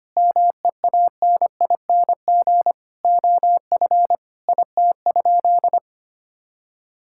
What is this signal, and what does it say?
Morse code, heard as a keyed tone, about 25 wpm, 700 Hz. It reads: MEANING OF IT?
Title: Sentences from Top 1000 Words - Review Part 6 - 25wpm